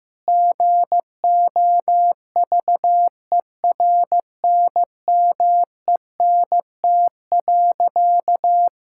Morse code, heard as a keyed tone, about 15 wpm, 700 Hz. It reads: GOVERNMENT.
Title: Sentences from Top 1000 Words - Review Part 6 - 15wpm